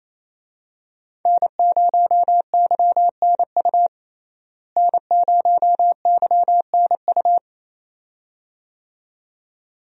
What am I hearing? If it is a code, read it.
Morse code, heard as a keyed tone, about 28 wpm, 700 Hz. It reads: N0YNU N0YNU
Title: USA Call Signs - 5 Characters - Repeated 2x - 28wpm